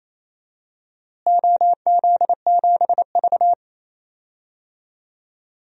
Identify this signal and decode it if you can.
Morse code, heard as a keyed tone, about 28 wpm, 700 Hz. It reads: OZ7V
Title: DXCC-100 Call Signs - 4 Characters - Repeated 3x - 28wpm